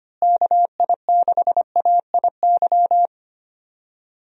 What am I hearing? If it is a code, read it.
Morse code, heard as a keyed tone, about 25 wpm, 700 Hz. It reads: KI6AIY
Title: USA Call Signs - 6 Characters - Repeated 3x - 25wpm